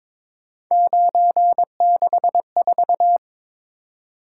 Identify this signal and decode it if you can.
Morse code, heard as a keyed tone, about 22 wpm, 700 Hz. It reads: 964
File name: practice / 22wpm / Three Letter-Number Combinations - 22wpm